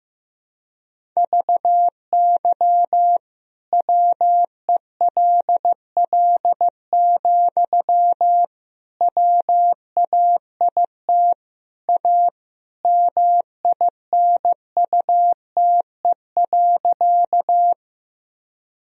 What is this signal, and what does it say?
Morse code, heard as a keyed tone, about 16 wpm, 700 Hz. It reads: VY WELL, WAIT A MINUTE.
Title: Sentences from Top 500 Words and Abbreviations 15wpm